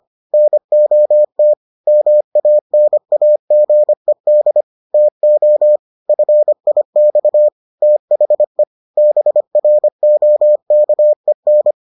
Morse code, heard as a keyed tone, about 25 wpm, 600 Hz. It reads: NOT MANAGED TO FIX THE BROKEN